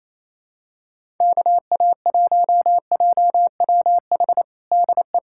Morse code, heard as a keyed tone, about 28 wpm, 700 Hz. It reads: KA1JWH DE
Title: USA Callsign DE Callsign - 6 Characters - 28wpm